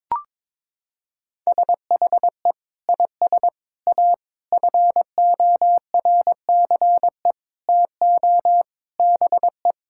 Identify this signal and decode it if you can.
Morse code, heard as a keyed tone, about 22 wpm, 700 Hz. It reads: SHE IS A FORCE TO BE
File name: practice / 22wpm / Sentences with 8-Word Phrases - 22wpm